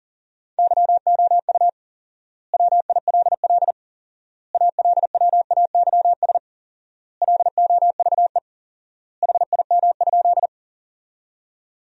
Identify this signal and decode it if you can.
Morse code, heard as a keyed tone, about 40 wpm, 700 Hz. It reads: YOU WILL ALWAYS LOVE HIM?
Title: 4x Word-Spacing - Sentences from Top 400 Words - 40wpm